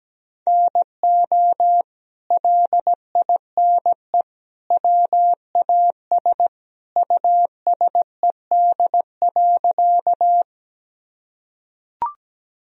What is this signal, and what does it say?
Morse code, heard as a keyed tone, about 17 wpm, 700 Hz. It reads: NO LINE WAS USED.
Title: Sentences from Top 1000 Words - Review Part 1 - 17wpm